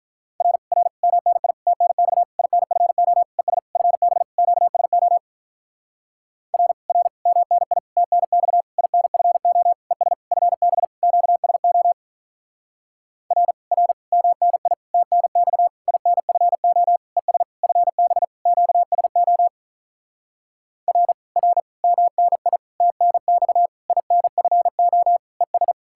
Morse code, heard as a keyed tone, about 44 wpm, 700 Hz. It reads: R R MNI TNX INFO ES FB QSO R R MNI TNX INFO ES FB QSO R R MNI TNX INFO ES FB QSO R R MNI TNX INFO ES